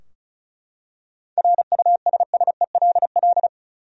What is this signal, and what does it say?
Morse code, heard as a keyed tone, about 36 wpm, 700 Hz. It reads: RUSSELL
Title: USA Common Names - 35wpm